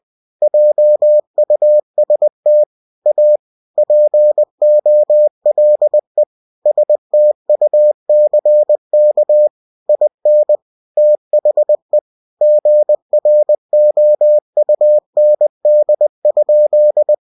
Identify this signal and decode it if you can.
Morse code, heard as a keyed tone, about 20 wpm, 600 Hz. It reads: JUST A POLE STUCK IN THE GROUND?
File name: cw-books / winnie-the-pooh / winnie-20wpm-page-0085